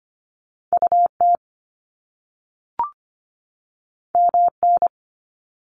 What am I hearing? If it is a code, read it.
Morse code, heard as a keyed tone, about 25 wpm, 700 Hz. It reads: UT MN
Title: US States 25wpm